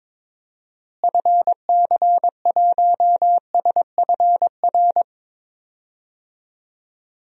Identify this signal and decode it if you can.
Morse code, heard as a keyed tone, about 22 wpm, 700 Hz. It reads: FC1SFR
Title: DXCC-100 Call Signs - 6 Characters - Repeated 3x - 22wpm